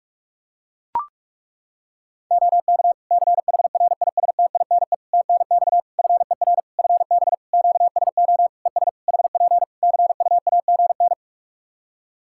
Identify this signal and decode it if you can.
Morse code, heard as a keyed tone, about 41 wpm, 700 Hz. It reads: OK CHRISTINE TNX FER FB QSO ES HP CUAGN
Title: Practice QSO - Exchange 4 - Ending - 45wpm